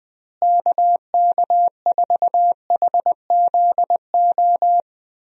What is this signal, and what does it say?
Morse code, heard as a keyed tone, about 20 wpm, 700 Hz. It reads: KK4HZO
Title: USA Call Signs - 6 Characters - Repeated 2x - 20wpm